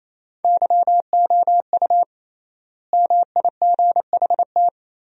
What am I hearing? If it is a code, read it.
Morse code, heard as a keyed tone, about 28 wpm, 700 Hz. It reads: YOU MIGHT
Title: 3x Word-Spacing - Sentences from Top 300 Words - 28wpm